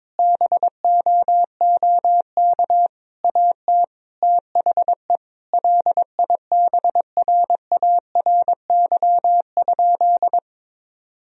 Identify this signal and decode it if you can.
Morse code, heard as a keyed tone, about 22 wpm, 700 Hz. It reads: BOOK AT THE LIBRARY?
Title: Sentences from Top 1000 Words - Review Part 4 - 22wpm